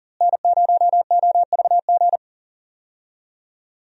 Morse code, heard as a keyed tone, about 40 wpm, 700 Hz. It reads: N0OVG
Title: USA Call Signs - 5 Characters - Repeated 3x - 40wpm